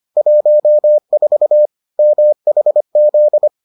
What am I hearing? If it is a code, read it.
Morse code, heard as a keyed tone, about 25 wpm, 600 Hz. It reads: 14 MHZ